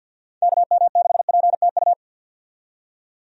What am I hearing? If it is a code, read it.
Morse code, heard as a keyed tone, about 49 wpm, 700 Hz. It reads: KM6PTU
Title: USA Call Signs - 6 Characters - 50wpm